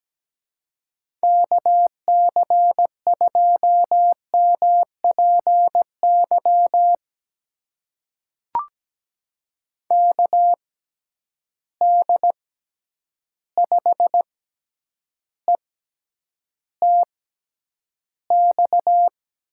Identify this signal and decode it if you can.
Morse code, heard as a keyed tone, about 17 wpm, spaced as for 9 wpm, 700 Hz. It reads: KC2MPY KD5ETX
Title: USA Call Signs - 6 Characters - 6x Character Spacing - 17wpm